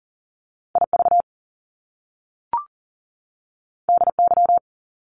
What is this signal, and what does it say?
Morse code, heard as a keyed tone, about 40 wpm, 700 Hz. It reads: IV DY
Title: Top Two Letter Combinations in Words 40wpm